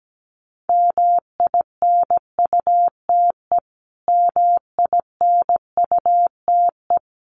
Morse code, heard as a keyed tone, about 17 wpm, 700 Hz. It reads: MINUTE MINUTE
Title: Top 500 Words Repeated 17wpm